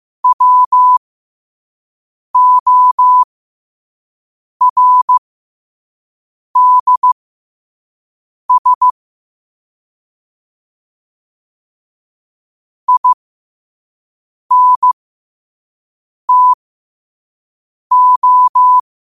Morse code, heard as a keyed tone, about 15 wpm, spaced as for 5 wpm, 1000 Hz. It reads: WORDS INTO
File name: qotdfiles / 5wpm